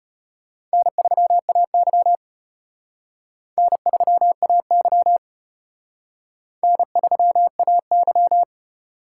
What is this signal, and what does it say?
Morse code, heard as a keyed tone, about 34 wpm, 700 Hz. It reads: N3AY N3AY N3AY